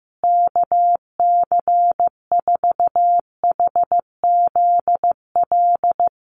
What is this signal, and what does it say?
Morse code, heard as a keyed tone, about 15 wpm, 700 Hz. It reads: KC4HZL
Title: USA Call Signs Repeated 3x 15wpm